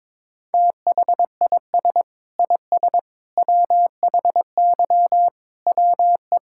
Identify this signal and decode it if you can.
Morse code, heard as a keyed tone, about 22 wpm, 700 Hz. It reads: THIS IS WHY WE
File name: practice / 22wpm / Sets of 8 Words - 22wpm